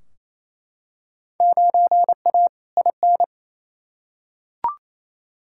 Morse code, heard as a keyed tone, about 28 wpm, 700 Hz. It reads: 9A IN
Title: Field Day Exchanges Encore - 28wpm